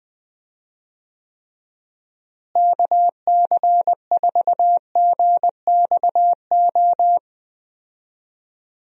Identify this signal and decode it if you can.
Morse code, heard as a keyed tone, about 21 wpm, 700 Hz. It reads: KC4GXO
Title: USA Call Signs - 6 Characters - Repeated 3x - 20wpm